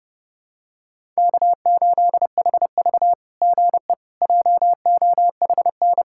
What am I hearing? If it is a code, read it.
Morse code, heard as a keyed tone, about 30 wpm, 690 Hz. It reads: K8HV GE JOHN
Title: SOTA QSO Practice 30wpm